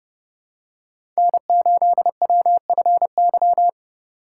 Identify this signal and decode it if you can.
Morse code, heard as a keyed tone, about 30 wpm, 700 Hz. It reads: N8WFY
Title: USA Call Signs - 5 Characters - 3x Character Spacing - 30wpm